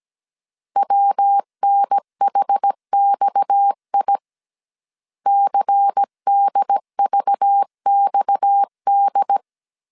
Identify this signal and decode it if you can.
Morse code, heard as a keyed tone, about 17 wpm, 800 Hz. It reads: WNHXI CDVXD
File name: 16wpm